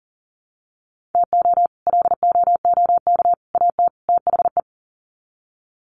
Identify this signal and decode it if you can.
Morse code, heard as a keyed tone, about 38 wpm, 700 Hz. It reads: TO LOOK AT THE
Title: Sets of 4 Words 40wpm